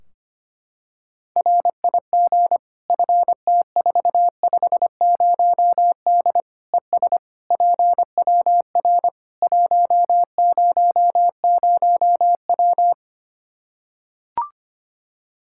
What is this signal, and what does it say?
Morse code, heard as a keyed tone, about 25 wpm, 700 Hz. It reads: RIG FT450D ES PWR 100W
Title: K1USN SST QSO Practice 25wpm